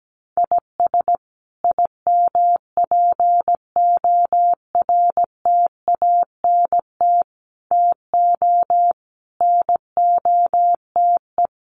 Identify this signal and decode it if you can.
Morse code, heard as a keyed tone, about 17 wpm, 700 Hz. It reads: IS IMPORTANT TO NOTE
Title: Sets of 5 Words 17wpm